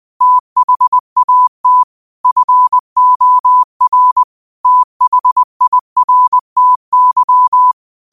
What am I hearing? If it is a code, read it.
Morse code, heard as a keyed tone, about 20 wpm, 1000 Hz. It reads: THAT FOR THIRTY
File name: qotdfiles / 20wpm